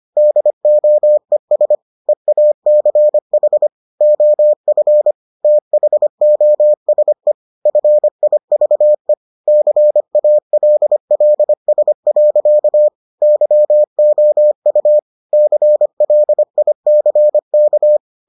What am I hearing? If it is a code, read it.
Morse code, heard as a keyed tone, about 25 wpm, 600 Hz. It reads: DOES EACH OF THOSE FIVE CALLS. YOU CLICK